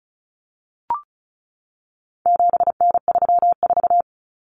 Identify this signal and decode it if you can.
Morse code, heard as a keyed tone, about 34 wpm, 700 Hz. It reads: 7N34
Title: Common Cut Numbers - 4-Digit 35wpm